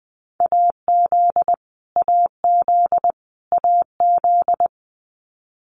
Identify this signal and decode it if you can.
Morse code, heard as a keyed tone, about 20 wpm, 700 Hz. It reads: AZ AZ AZ
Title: US States Repeated 3x 20wpm